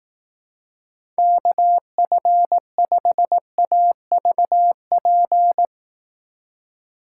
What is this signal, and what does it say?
Morse code, heard as a keyed tone, about 18 wpm, 700 Hz. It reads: KF5AVP
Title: Speed Racing - USA 2x3 Call Signs 15wpm